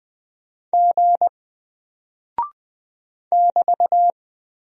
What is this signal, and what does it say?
Morse code, heard as a keyed tone, about 20 wpm, 700 Hz. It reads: G <BT>
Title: Single Character (Letter:Number:Punctuation:Prosign) 20wpm